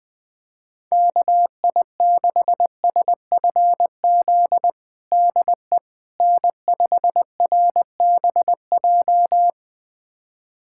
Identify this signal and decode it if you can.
Morse code, heard as a keyed tone, about 21 wpm, 700 Hz. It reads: KI6SFZ DE N5RBJ
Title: USA Callsign DE Callsign 20wpm